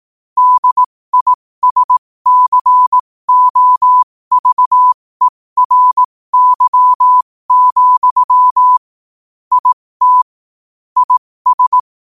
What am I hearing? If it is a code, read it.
Morse code, heard as a keyed tone, about 18 wpm, spaced as for 15 wpm, 1000 Hz. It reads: DISCOVERY, IT IS
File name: qotdfiles / 15wpm